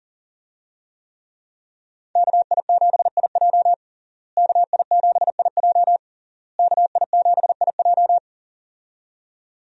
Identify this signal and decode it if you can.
Morse code, heard as a keyed tone, about 40 wpm, 700 Hz. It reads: KI7IJ KI7IJ KI7IJ
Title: USA Call Signs - 5 Characters - Repeated 3x - 40wpm